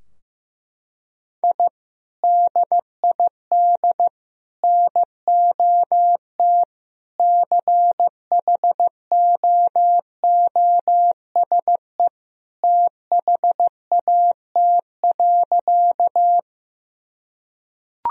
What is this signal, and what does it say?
Morse code, heard as a keyed tone, about 15 wpm, 700 Hz. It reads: I DID NOT CHOOSE THAT.
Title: Sentences from Top 1000 Words - Review Part 4 - 15wpm